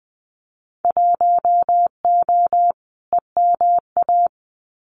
Morse code, heard as a keyed tone, about 20 wpm, 700 Hz. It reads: 1O EMA
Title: Winter Field Day Exchanges 20wpm